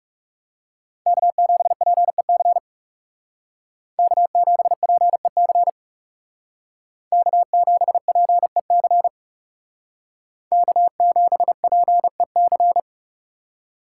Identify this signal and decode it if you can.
Morse code, heard as a keyed tone, about 38 wpm, 700 Hz. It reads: K7PEC K7PEC K7PEC K7PEC